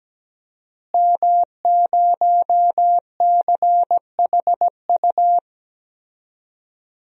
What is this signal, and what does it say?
Morse code, heard as a keyed tone, about 17 wpm, 700 Hz. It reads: M0CHU
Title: DXCC-100 Call Signs - 5 Characters - Repeated 2x - 17wpm